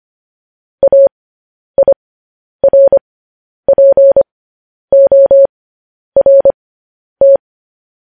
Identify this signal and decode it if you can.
Morse code, heard as a keyed tone, about 25 wpm, spaced as for 10 wpm, 550 Hz. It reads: AIRPORT
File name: podcast / News headlines-1x-25x10